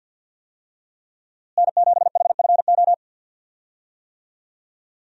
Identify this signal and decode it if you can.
Morse code, heard as a keyed tone, about 48 wpm, 700 Hz. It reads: N7HFO
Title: USA Call Signs - 5 Characters - Repeated 3x - 50wpm